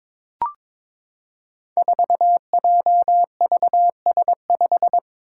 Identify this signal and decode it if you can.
Morse code, heard as a keyed tone, about 22 wpm, 700 Hz. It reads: 4JVS5
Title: Five Challenging Letter-Number Combinations 22wpm